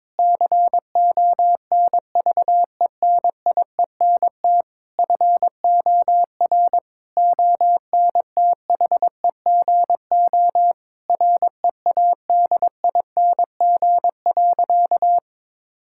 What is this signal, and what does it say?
Morse code, heard as a keyed tone, about 22 wpm, 700 Hz. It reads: CONVENIENT FOR ONTHEGO READING.
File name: Difficult Sentences with Hints - 22wpm